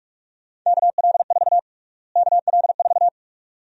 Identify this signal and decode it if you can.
Morse code, heard as a keyed tone, about 42 wpm, 700 Hz. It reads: KL4 KL4
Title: USA Call Sign Prefixes - Repeated 2x - 45wpm